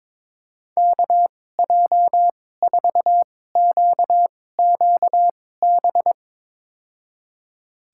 Morse code, heard as a keyed tone, about 22 wpm, spaced as for 16 wpm, 700 Hz. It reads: KJ4QQB
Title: USA Call Signs - 2x Character Spacing - 22wpm